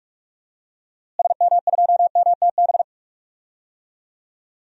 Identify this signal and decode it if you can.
Morse code, heard as a keyed tone, about 44 wpm, 700 Hz. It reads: SM2MTB
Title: DXCC-100 Call Signs - 6 Characters - Repeated 2x - 45wpm